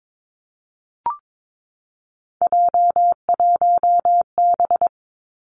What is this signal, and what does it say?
Morse code, heard as a keyed tone, about 22 wpm, 700 Hz. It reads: J1B
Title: Three Challenging Letter-Number Combinations 22wpm